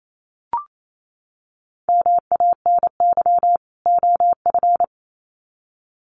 Morse code, heard as a keyed tone, about 28 wpm, 700 Hz. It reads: MANY OF
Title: Sets of 2 Words 28wpm